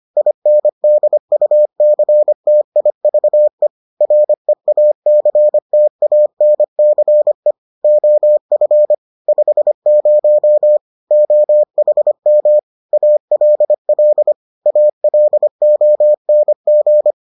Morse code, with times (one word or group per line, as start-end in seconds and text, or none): 0.2-3.7 INDUCTIVE
4.0-7.5 REACTANCE
7.8-8.9 OF
9.3-10.8 50
11.1-12.6 OHM
12.9-14.3 ALL
14.7-17.1 ALONG